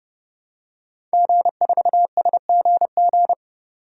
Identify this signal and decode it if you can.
Morse code, heard as a keyed tone, about 30 wpm, 700 Hz. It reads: G4SGG
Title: DXCC-100 Call Signs - 5 Characters - 30wpm